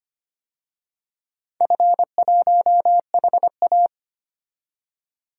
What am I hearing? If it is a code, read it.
Morse code, heard as a keyed tone, about 25 wpm, 700 Hz. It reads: F1HA
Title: DXCC-100 Call Signs - 4 Characters - 25wpm